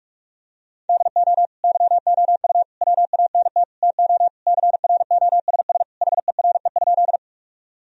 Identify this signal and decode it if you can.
Morse code, heard as a keyed tone, about 46 wpm, 700 Hz. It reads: DO YOU WANT TO CROSS HERE?
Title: Sentences from Top 1000 Words - Review Part 1 - 45wpm